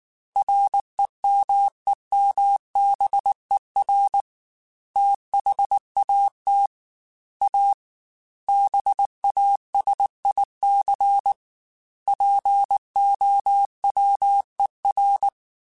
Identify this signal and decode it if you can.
Morse code, heard as a keyed tone, about 19 wpm, 800 Hz. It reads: REMEMBER THAT A BASIC POWER
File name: Technical Subjects / Technical Subjects-mp3s / Sw Pwr Sup @18